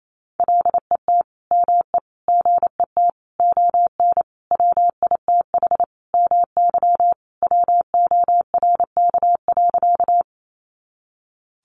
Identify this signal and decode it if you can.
Morse code, heard as a keyed tone, about 28 wpm, 700 Hz. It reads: LET ME GET ON WITH MY WORK.
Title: Sentences from Top 300 Words 28wpm